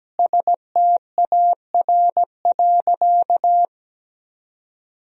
Morse code, heard as a keyed tone, about 17 wpm, 700 Hz. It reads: STAR.
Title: Sentences from Top 1000 Words - Review Part 2 - 17wpm